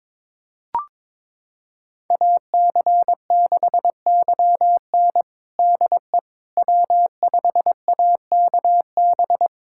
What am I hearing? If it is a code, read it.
Morse code, heard as a keyed tone, about 22 wpm, 700 Hz. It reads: AC6YN DE W5AKB
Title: USA Callsign DE Callsign - 5 Characters - 22wpm